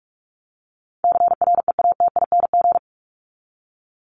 Morse code, heard as a keyed tone, about 44 wpm, 700 Hz. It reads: CREATING
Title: ING Suffix - Top 100 Words with Base Words 45wpm